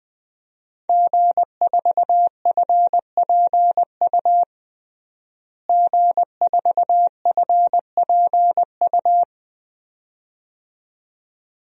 Morse code, heard as a keyed tone, about 20 wpm, 700 Hz. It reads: G4FPU G4FPU
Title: DXCC-100 Call Signs - 5 Characters - Repeated 2x - 20wpm